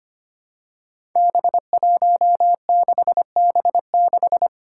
Morse code, heard as a keyed tone, about 25 wpm, 700 Hz. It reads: B16B6
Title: Five Challenging Letter-Number Combinations 25wpm